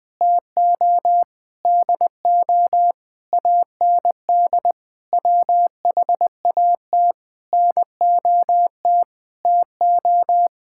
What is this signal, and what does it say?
Morse code, heard as a keyed tone, about 20 wpm, 700 Hz. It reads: TO DO AND WHAT NOT TO
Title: Sentences with 8-Word Phrases - Spoken Beginning - 20wpm